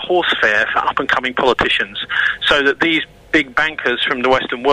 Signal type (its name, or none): Emergency Alert System (SAME) tones